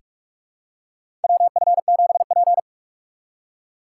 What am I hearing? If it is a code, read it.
Morse code, heard as a keyed tone, about 46 wpm, 700 Hz. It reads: WF7P
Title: USA Call Signs - 4 Characters - 6x Character Spacing - 45wpm